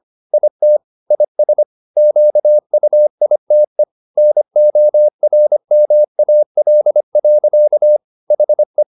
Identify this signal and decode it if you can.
Morse code, heard as a keyed tone, about 25 wpm, 600 Hz. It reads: IT IS QUITE NORMAL. HE
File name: the old man and the sea part 1 25wpm